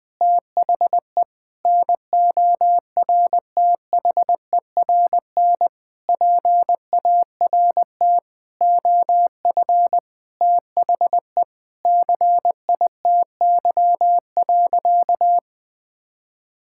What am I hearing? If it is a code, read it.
Morse code, heard as a keyed tone, about 20 wpm, 700 Hz. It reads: THE NORTHERN PART OF THE CITY.
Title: Sentences with 6-Word Phrases - 20wpm